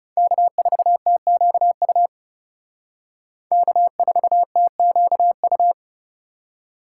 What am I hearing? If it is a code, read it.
Morse code, heard as a keyed tone, about 36 wpm, 700 Hz. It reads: K4TQU K4TQU
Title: Speed Racing - USA 1x3 Call Signs 30wpm